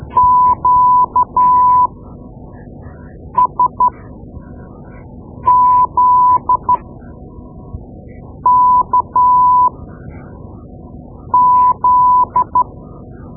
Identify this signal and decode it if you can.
Morse code, heard as a keyed tone, about 12 wpm, spaced as for 5 wpm, 1000 Hz. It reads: QSZKZ
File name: Morse Lessons / Morse Lessons / Tape-2B